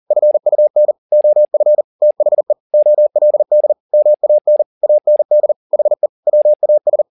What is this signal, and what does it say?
Morse code, heard as a keyed tone, about 40 wpm, 600 Hz. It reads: FUN OF THE OLD MAN AND HE WAS